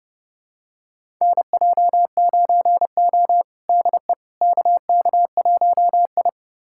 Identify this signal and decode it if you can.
Morse code, heard as a keyed tone, about 30 wpm, 700 Hz. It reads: NJ9O DE KK1I
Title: USA Callsign DE Callsign - 4 Characters - 30wpm